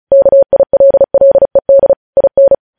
Morse code, heard as a keyed tone, about 36 wpm, 550 Hz. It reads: KILLED IN
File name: podcast / News headlines-1x-35x35